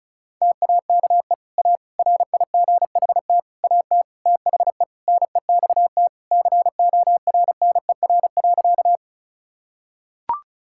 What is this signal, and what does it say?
Morse code, heard as a keyed tone, about 36 wpm, 700 Hz. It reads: TAKE A RIGHT AT THE NEXT CORNER.
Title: Sentences from Top 1600 Words 35wpm